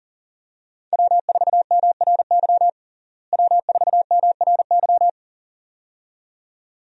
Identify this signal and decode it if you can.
Morse code, heard as a keyed tone, about 40 wpm, 700 Hz. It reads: W4MRY W4MRY
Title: USA Call Signs - 5 Characters - Repeated 2x - 40wpm